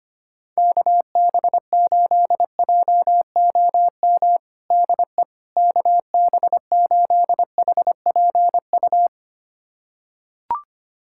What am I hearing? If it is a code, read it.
Morse code, heard as a keyed tone, about 25 wpm, 700 Hz. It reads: KB8JOM DE KB8HPU
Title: USA Callsign DE Callsign - 6 Characters - 25wpm